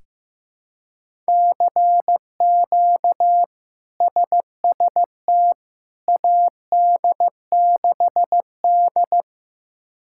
Morse code, heard as a keyed tone, about 15 wpm, 700 Hz. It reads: CQ SST AD6D
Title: K1USN SST QSO Practice - 15wpm